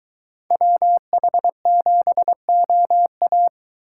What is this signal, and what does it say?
Morse code, heard as a keyed tone, about 23 wpm, 700 Hz. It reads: WH7OA